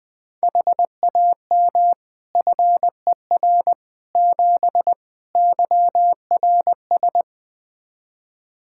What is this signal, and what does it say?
Morse code, heard as a keyed tone, about 20 wpm, 700 Hz. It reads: HAM FER 7 YRS
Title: Practice QSO - Exchange 3 - More Info - 20wpm